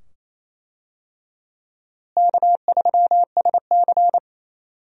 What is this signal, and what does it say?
Morse code, heard as a keyed tone, about 28 wpm, 700 Hz. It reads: K3SC
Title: USA Call Signs - 4 Characters - Repeated 2x - 28wpm